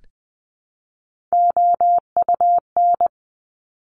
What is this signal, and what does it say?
Morse code, heard as a keyed tone, about 20 wpm, 700 Hz. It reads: OUN
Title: Top Three Letter Combinations in Words 20wpm